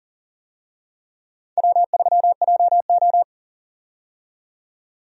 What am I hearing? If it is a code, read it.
Morse code, heard as a keyed tone, about 40 wpm, 700 Hz. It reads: W3JO
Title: USA Call Signs - 4 Characters - 40wpm